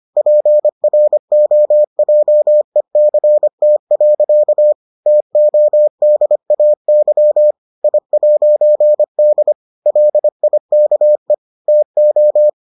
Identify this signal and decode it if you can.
Morse code, heard as a keyed tone, about 25 wpm, 600 Hz. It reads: PROJECT. TODAY I'D LIKE TO